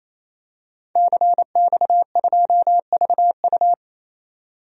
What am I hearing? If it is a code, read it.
Morse code, heard as a keyed tone, about 28 wpm, 700 Hz. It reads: CX2VU
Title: DXCC-100 Call Signs - 5 Characters - 28wpm